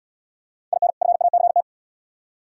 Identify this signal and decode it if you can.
Morse code, heard as a keyed tone, about 73 wpm, 690 Hz. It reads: IN FACT